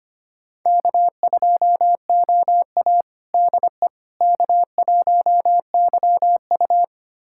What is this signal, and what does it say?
Morse code, heard as a keyed tone, about 25 wpm, 700 Hz. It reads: K2OA DE K1YU
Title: USA Callsign DE Callsign - 4 Characters - 25wpm